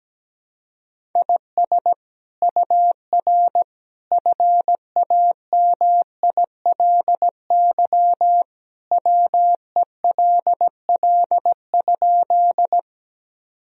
Morse code, heard as a keyed tone, about 17 wpm, 700 Hz. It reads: IS UR FAMILY WELL?
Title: Sentences from Top 400 Words and Abbreviations 17wpm